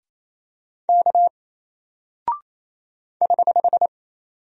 Morse code, heard as a keyed tone, about 27 wpm, 700 Hz. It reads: K <HH>
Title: Single Character (Letter:Number:Punctuation:Prosign) Alt 28wpm